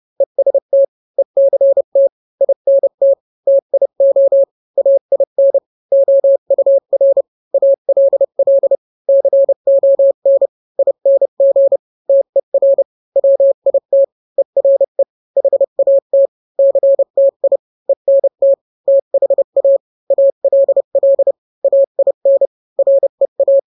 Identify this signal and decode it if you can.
Morse code, heard as a keyed tone, about 30 wpm, 550 Hz. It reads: EST ECT INT TIO AIN OUR ALL CON ING TER WIT ERE HAT CTI ENT THA ALL AIN REA